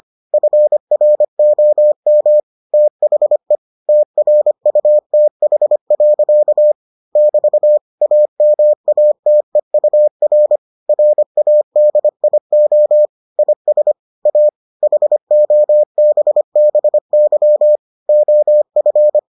Morse code, heard as a keyed tone, about 25 wpm, 600 Hz. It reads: FROM THE TRUTH. <BT> AMATEUR RADIO IS A HOBBY OF